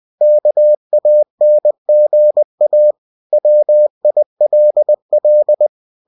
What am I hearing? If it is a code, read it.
Morse code, heard as a keyed tone, about 20 wpm, 600 Hz. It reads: KANGA WILL